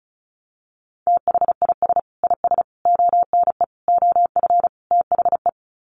Morse code, heard as a keyed tone, about 34 wpm, 700 Hz. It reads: THIS IS ONE OF THE
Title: Sets of 5 Words 35wpm